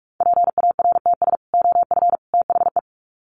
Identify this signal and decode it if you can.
Morse code, heard as a keyed tone, about 44 wpm, 700 Hz. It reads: PARTS OF THE
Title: Sets of 3 Words 45wpm